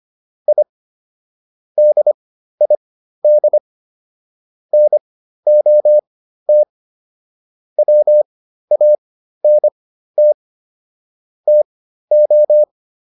Morse code, heard as a keyed tone, about 25 wpm, spaced as for 13 wpm, 600 Hz. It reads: I DID NOT WANT TO